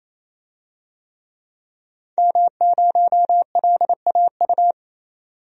Morse code, heard as a keyed tone, about 28 wpm, 700 Hz. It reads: M0LAU
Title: DXCC-100 Call Signs - 5 Characters - 28wpm